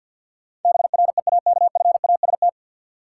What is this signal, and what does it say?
Morse code, heard as a keyed tone, about 51 wpm, 700 Hz. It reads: BREAKFAST
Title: Sentences from Top 1500 Words - New Word Drill - 50wpm